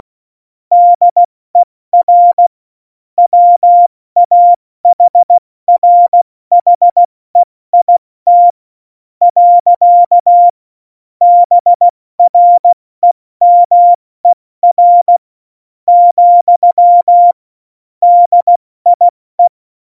Morse code, heard as a keyed tone, about 16 wpm, spaced as for 13 wpm, 700 Hz. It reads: DER WAHRHEIT. BREMER, DIE
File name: Morsekurs / Witze / Witz 05. Januar 2021, 70 F80